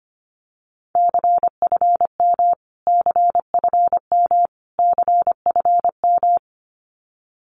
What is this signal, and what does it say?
Morse code, heard as a keyed tone, about 25 wpm, 700 Hz. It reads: CFM CFM CFM
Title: Abbreviations Repeated 3x - 25wpm